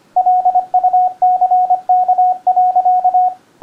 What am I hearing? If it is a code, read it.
Morse code, heard as a keyed tone, about 25 wpm, 730 Hz, with radio noise. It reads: LUCK.